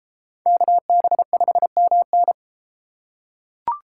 Morse code, heard as a keyed tone, about 31 wpm, 700 Hz. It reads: KB5MN